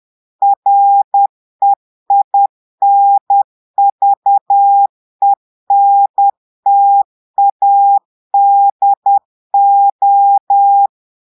Morse code, heard as a keyed tone, about 10 wpm, 800 Hz. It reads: REINVENTADO
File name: portugais texte en clair 2 vitesse 10 mots minutes